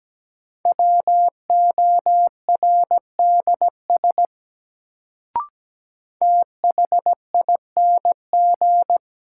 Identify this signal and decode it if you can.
Morse code, heard as a keyed tone, about 17 wpm, 700 Hz. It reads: WORDS THING
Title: Sets of 3 Words - New Word Drill - 17wpm